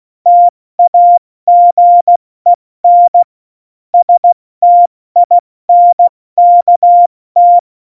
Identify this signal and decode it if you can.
Morse code, heard as a keyed tone, about 16 wpm, 700 Hz. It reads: TAGEN STINKT
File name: Morsekurs / Witze / Witz 23. Oktober 2018, 70 F80